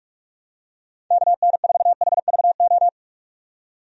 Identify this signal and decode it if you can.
Morse code, heard as a keyed tone, about 46 wpm, 700 Hz. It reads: KN4HVO
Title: USA Call Signs - 6 Characters - 45wpm